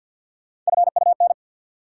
Morse code, heard as a keyed tone, about 53 wpm, 700 Hz. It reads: FUN
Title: Sentences from Top 700 Words - New Word Drill - 50wpm